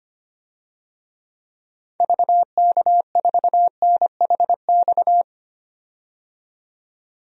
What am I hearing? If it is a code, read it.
Morse code, heard as a keyed tone, about 25 wpm, 700 Hz. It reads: VK4NHX
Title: DXCC-100 Call Signs - 6 Characters - 25wpm